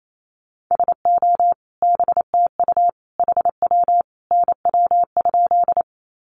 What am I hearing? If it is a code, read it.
Morse code, heard as a keyed tone, about 28 wpm, 700 Hz. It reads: SO BTU HW NW?
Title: Common QSO Phrases 28wpm